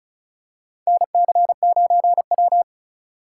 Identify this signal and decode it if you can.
Morse code, heard as a keyed tone, about 36 wpm, 700 Hz. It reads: NC9W
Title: K1USN SST QSO Practice - 35wpm